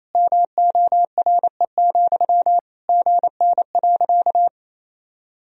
Morse code, heard as a keyed tone, about 28 wpm, 700 Hz. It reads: MORE, GN.
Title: Sentences from Top 300 Words and Abbreviations 28wpm